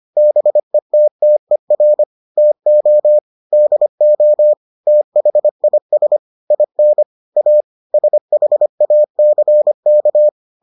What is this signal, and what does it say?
Morse code, heard as a keyed tone, about 25 wpm, 600 Hz. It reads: BETTER TO DO THIS IN A SHACK